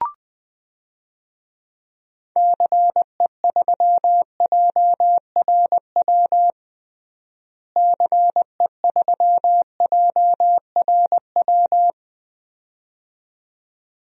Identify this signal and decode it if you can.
Morse code, heard as a keyed tone, about 20 wpm, 700 Hz. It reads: CE3JRW CE3JRW
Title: DXCC-100 Call Signs - 6 Characters - Repeated 2x - 20wpm